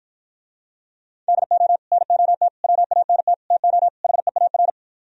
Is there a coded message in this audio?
Yes: Morse code from 1.3 to 4.7 s, heard as a keyed tone, about 52 wpm, 700 Hz. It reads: DO NOT WANT TO HEAR